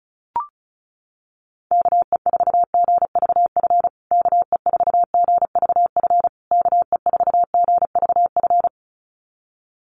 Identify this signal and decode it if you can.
Morse code, heard as a keyed tone, about 34 wpm, 700 Hz. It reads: KE4GVF KE4GVF KE4GVF
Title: USA Call Signs Repeated 3x 35wpm